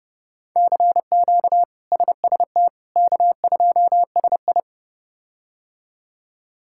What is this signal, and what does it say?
Morse code, heard as a keyed tone, about 30 wpm, 700 Hz. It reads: CQ SST K2SI
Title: K1USN SST QSO Practice - 30wpm